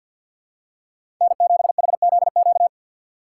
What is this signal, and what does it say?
Morse code, heard as a keyed tone, about 52 wpm, 700 Hz. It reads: N7HZQ
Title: USA Call Signs - 5 Characters - 3x Character Spacing - 50wpm